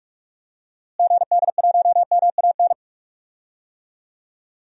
Morse code, heard as a keyed tone, about 41 wpm, 700 Hz. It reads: GD1MAN
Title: DXCC-100 Call Signs - 6 Characters - Repeated 3x - 45wpm